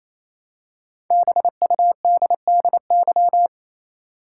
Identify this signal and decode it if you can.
Morse code, heard as a keyed tone, about 28 wpm, 700 Hz. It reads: BUDDY
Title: Five Letter Words - 28wpm